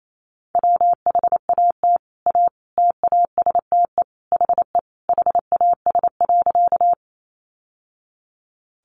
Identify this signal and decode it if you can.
Morse code, heard as a keyed tone, about 28 wpm, 700 Hz. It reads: WHAT A TASTE HE HAS.
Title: Sentences from Top 1100 Words 28wpm